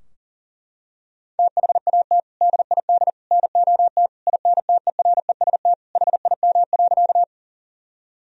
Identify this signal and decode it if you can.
Morse code, heard as a keyed tone, about 40 wpm, 700 Hz. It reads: THAT DID NOT INTEREST HIM.
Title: Sentences from Top 1000 Words - Review Part 2 - 40wpm